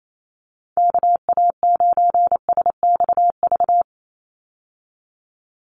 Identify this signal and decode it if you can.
Morse code, heard as a keyed tone, about 28 wpm, 700 Hz. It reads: KA9SXV
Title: Call Signs Repeated 3x - 28wpm